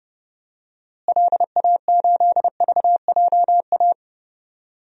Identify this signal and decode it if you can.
Morse code, heard as a keyed tone, about 30 wpm, 700 Hz. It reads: LA8VJA
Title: DXCC-100 Call Signs - 6 Characters - Repeated 3x - 30wpm